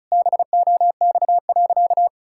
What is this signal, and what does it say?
Morse code, heard as a keyed tone, about 36 wpm, 690 Hz. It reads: BOX.
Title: Sentences from Top 1000 Words - Review Part 6 - 35wpm